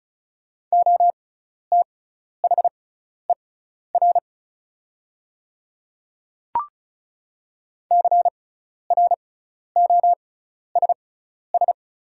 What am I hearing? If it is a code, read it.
Morse code, heard as a keyed tone, about 36 wpm, spaced as for 12 wpm, 700 Hz. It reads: OTHER CROSS
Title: 6x Character Spacing - Top 500 Words - 35wpm